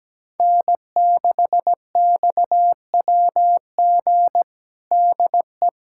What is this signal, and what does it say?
Morse code, heard as a keyed tone, about 17 wpm, 700 Hz. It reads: N6XWG DE
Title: USA Callsign DE Callsign - 5 Characters - 17wpm